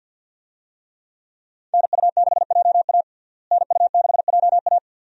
Morse code, heard as a keyed tone, about 47 wpm, 700 Hz. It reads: NU6JA NU6JA
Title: USA Call Signs - 5 Characters - Repeated 2x - 50wpm